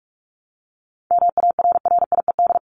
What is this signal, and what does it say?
Morse code, heard as a keyed tone, about 44 wpm, 700 Hz. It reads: MARRIED
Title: ED Suffix - Top 100 Words with Base Words 45wpm